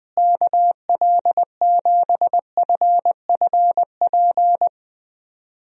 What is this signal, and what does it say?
Morse code, heard as a keyed tone, about 20 wpm, 700 Hz. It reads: KL7FFP
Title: USA Call Signs - 6 Characters - Repeated 3x - 20wpm